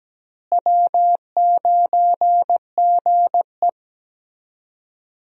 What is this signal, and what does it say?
Morse code, heard as a keyed tone, about 17 wpm, 700 Hz. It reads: W9GE
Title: USA Call Signs - 4 Characters - Repeated 2x - 17wpm